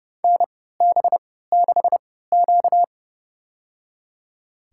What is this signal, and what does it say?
Morse code, heard as a keyed tone, about 30 wpm, spaced as for 17 wpm, 700 Hz. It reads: NB6Q
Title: USA Call Signs - 4 Characters - 3x Character Spacing - 30wpm